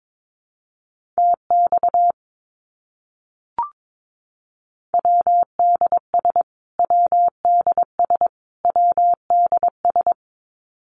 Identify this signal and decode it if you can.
Morse code, heard as a keyed tone, about 22 wpm, 700 Hz. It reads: TX WDS WDS WDS
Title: Abbreviations Repeated 3x - 22wpm